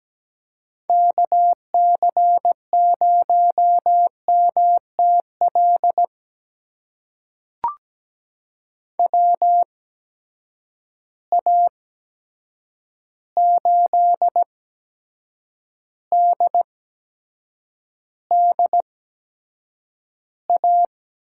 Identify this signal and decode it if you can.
Morse code, heard as a keyed tone, about 17 wpm, spaced as for 7 wpm, 700 Hz. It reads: KC0MTL WA8DDA
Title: USA Call Signs - 6 Characters - 8x Character Spacing - 17wpm